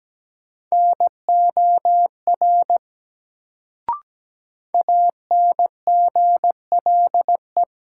Morse code, heard as a keyed tone, about 17 wpm, 700 Hz. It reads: NOR ANGLE
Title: Sentences from Top 1500 Words - New Word Drill - 17wpm